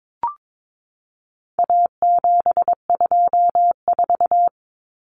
Morse code, heard as a keyed tone, about 22 wpm, 700 Hz. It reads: A724
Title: Common Cut Numbers - 4-Digit 22wpm